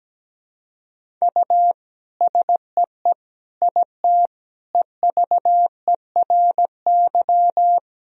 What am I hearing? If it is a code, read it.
Morse code, heard as a keyed tone, about 17 wpm, 700 Hz. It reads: U SEE IT EVERY
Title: Sentences from Top 200 Words and Abbreviations 17wpm